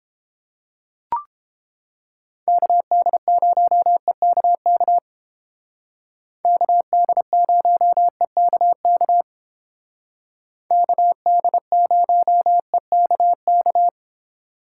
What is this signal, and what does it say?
Morse code, heard as a keyed tone, about 31 wpm, 700 Hz. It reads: KD0EKK KD0EKK KD0EKK